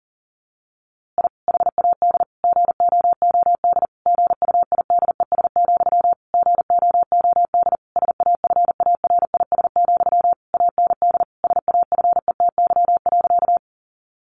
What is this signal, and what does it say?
Morse code, heard as a keyed tone, about 40 wpm, 700 Hz. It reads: I HAD GOOD GUIDES, GOOD SAFARIS, AND SAFETY.
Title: A Sound Of Thunder 40wpm